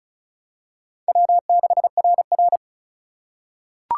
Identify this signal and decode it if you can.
Morse code, heard as a keyed tone, about 34 wpm, 700 Hz. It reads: W6RR
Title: USA Call Signs - 4 Characters - 6x Character Spacing - 35wpm